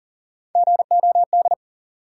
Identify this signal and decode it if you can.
Morse code, heard as a keyed tone, about 40 wpm, 700 Hz. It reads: GOD
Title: Three Letter Words - 40wpm